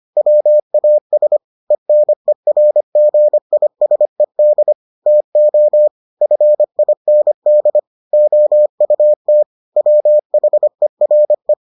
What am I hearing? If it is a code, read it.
Morse code, heard as a keyed tone, about 25 wpm, 600 Hz. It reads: WAS ENERGISED TO FIND OUT WHERE